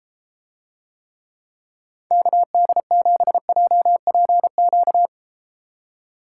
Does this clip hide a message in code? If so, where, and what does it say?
Yes: Morse code from 2.1 to 5.1 s, heard as a keyed tone, about 33 wpm, 700 Hz. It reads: KD7JPQ